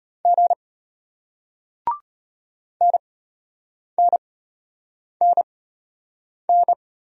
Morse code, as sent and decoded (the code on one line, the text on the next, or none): --. / -. -. -. -.
G NNNN